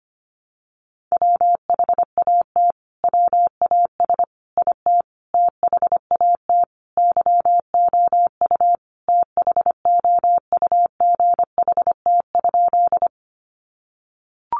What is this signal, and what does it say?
Morse code, heard as a keyed tone, about 25 wpm, 700 Hz. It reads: WHAT WAS IT THAT YOU THOUGHT?
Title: Sentences from Top 300 Words 25wpm